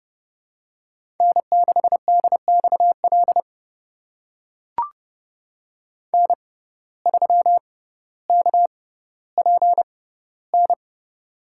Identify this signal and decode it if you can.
Morse code, heard as a keyed tone, about 30 wpm, spaced as for 15 wpm, 700 Hz. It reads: N6DXL N3KPN
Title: USA Call Signs - 6x Character Spacing - 30wpm